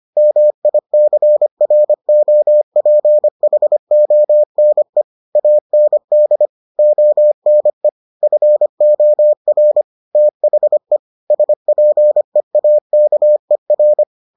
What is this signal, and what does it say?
Morse code, heard as a keyed tone, about 25 wpm, 600 Hz. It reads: MICROPHONE AND ONE FOR THE SPEAKER